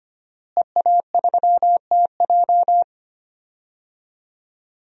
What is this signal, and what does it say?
Morse code, heard as a keyed tone, about 25 wpm, 700 Hz. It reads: EA3TJ
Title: DXCC-100 Call Signs - 5 Characters - Repeated 3x - 25wpm